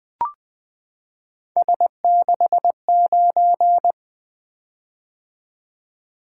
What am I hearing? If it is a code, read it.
Morse code, heard as a keyed tone, about 20 wpm, 700 Hz. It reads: S69
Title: Three Letter-Number Combinations - 20wpm